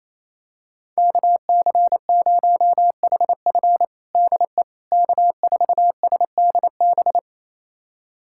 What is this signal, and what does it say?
Morse code, heard as a keyed tone, about 28 wpm, 700 Hz. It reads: KC0HF DE K4SDB
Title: USA Callsign DE Callsign - 5 Characters - 28wpm